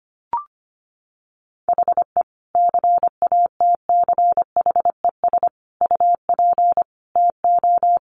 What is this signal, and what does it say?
Morse code, heard as a keyed tone, about 25 wpm, 700 Hz. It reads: HE CATCHES UP TO
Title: Additional Easy Phrases 25wpm